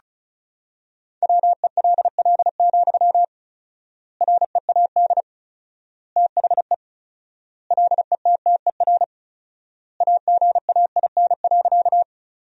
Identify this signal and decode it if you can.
Morse code, heard as a keyed tone, about 34 wpm, 700 Hz. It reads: WELL, READ THE LETTER AGAIN.
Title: 4x Word-Spacing - Sentences from Top 300 Words - 35wpm